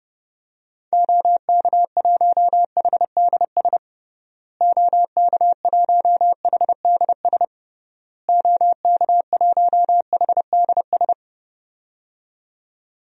Morse code, heard as a keyed tone, about 30 wpm, 700 Hz. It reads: OK1HDS OK1HDS OK1HDS
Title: DXCC-100 Call Signs - 6 Characters - Repeated 3x - 30wpm